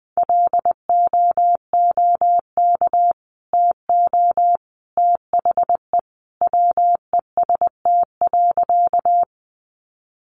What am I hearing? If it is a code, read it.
Morse code, heard as a keyed tone, about 20 wpm, 700 Hz. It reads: LOOK TO THE WEST.
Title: Sentences from Top 400 Words 20wpm